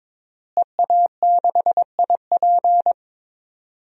Morse code, heard as a keyed tone, about 21 wpm, 700 Hz. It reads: EA6IP
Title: DXCC-100 Call Signs - 5 Characters - Repeated 3x - 22wpm